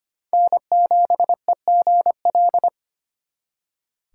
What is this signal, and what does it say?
Morse code, heard as a keyed tone, about 32 wpm, 700 Hz. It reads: N7EGL